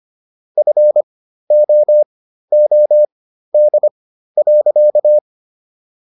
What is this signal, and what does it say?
Morse code, heard as a keyed tone, about 25 wpm, spaced as for 13 wpm, 600 Hz. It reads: FOOD.